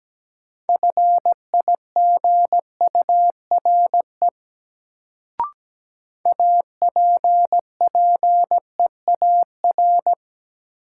Morse code, heard as a keyed tone, about 17 wpm, 700 Hz. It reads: FIGURE APPEAR
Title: Sentences from Top 1000 Words - Review Part 2 - New Word Drill - 17wpm